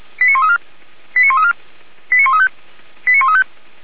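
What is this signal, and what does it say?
Five-tone selective call (ZVEI), heard as quick runs of five tones, tones of about 70 ms. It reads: 89134 89134 89135 89135
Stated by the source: ZVEI five-tone selective call received on an SDR by the fire brigade feuerwehr-orlamuende.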